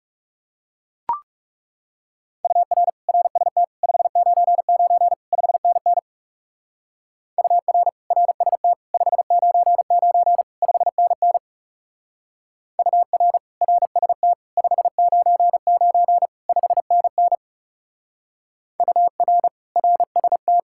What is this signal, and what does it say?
Morse code, heard as a keyed tone, about 38 wpm, 700 Hz. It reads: UR RST 599 5NN UR RST 599 5NN UR RST 599 5NN UR RST